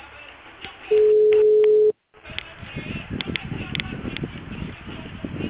Ringback tone (Europe, 425 Hz).